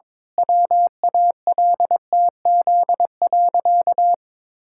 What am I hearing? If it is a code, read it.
Morse code, heard as a keyed tone, about 21 wpm, 700 Hz. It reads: WALTZ.
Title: Pangram Challenge v2 - 22wpm